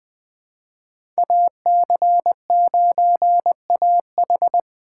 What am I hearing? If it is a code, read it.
Morse code, heard as a keyed tone, about 20 wpm, 700 Hz. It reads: AC9AH
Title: USA Call Signs - 2x Character Spacing - 20wpm